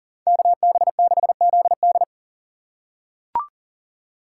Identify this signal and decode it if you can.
Morse code, heard as a keyed tone, about 40 wpm, 700 Hz. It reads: KB6ZD
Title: USA Call Signs - 5 Characters - 3x Character Spacing - 40wpm